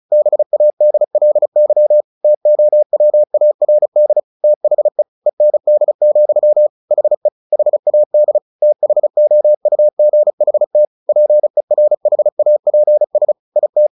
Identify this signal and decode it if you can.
Morse code, heard as a keyed tone, about 35 wpm, 600 Hz. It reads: BADLY TOWARD THE END, HE HAD THOUGHT PERHAPS IT
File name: the old man and the sea 2 35wpm